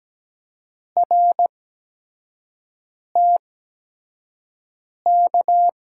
Morse code, heard as a keyed tone, about 17 wpm, spaced as for 5 wpm, 700 Hz. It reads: RTK
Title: USA Call Signs - 6 Characters - 8x Character Spacing - 17wpm